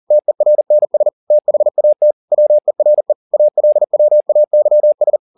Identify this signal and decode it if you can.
Morse code, heard as a keyed tone, about 40 wpm, 600 Hz. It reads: TERNS THAT WERE ALWAYS